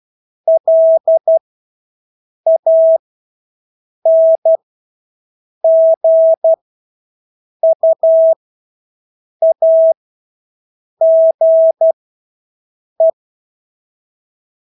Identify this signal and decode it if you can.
Morse code, heard as a keyed tone, about 12 wpm, spaced as for 6 wpm, 650 Hz. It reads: LANGUAGE
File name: Engelske ord 12-6 wpm